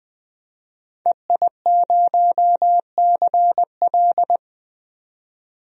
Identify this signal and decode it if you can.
Morse code, heard as a keyed tone, about 20 wpm, 700 Hz. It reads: EI0CL
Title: DXCC-100 Call Signs - 5 Characters - 20wpm